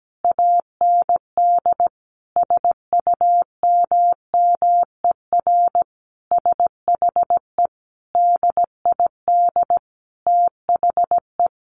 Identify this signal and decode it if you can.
Morse code, heard as a keyed tone, about 17 wpm, 700 Hz. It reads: AND SUMMER SHE DID THE
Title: Sentences from Top 1200 Words 17wpm